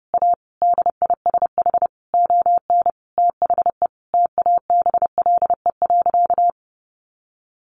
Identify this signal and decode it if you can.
Morse code, heard as a keyed tone, about 30 wpm, 700 Hz. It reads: A DISH ON THE TABLE.
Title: Sentences from Top 1300 Words 30wpm